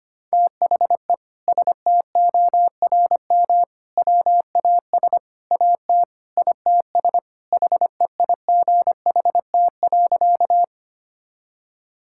Morse code, heard as a keyed tone, about 25 wpm, 700 Hz. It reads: THE STORM WAS AT ITS HEIGHT.
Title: Sentences from Top 1600 Words 25wpm